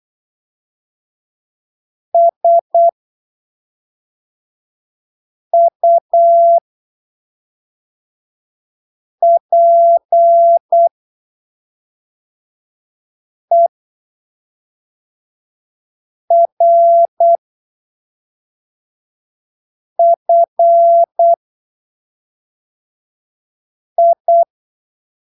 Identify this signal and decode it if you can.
Morse code, heard as a keyed tone, about 8 wpm, spaced as for 3 wpm, 680 Hz. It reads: SUPERFI
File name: arquivos / texto / texto 10 08ppm